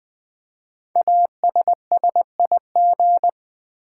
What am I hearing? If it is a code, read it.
Morse code, heard as a keyed tone, about 20 wpm, 700 Hz. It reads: ASSIG
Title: Build A Word - 5 Letters At A Time - 10-Letter Words - 20wpm